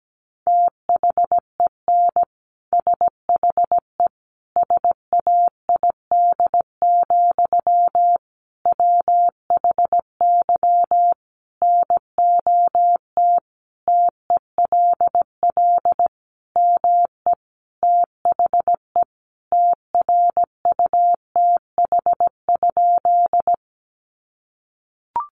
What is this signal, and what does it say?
Morse code, heard as a keyed tone, about 17 wpm, 700 Hz. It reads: THEN SHE SAID, WHY NOT TELL ME THE TRUTH?
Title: Sentences from Top 800 Words 17wpm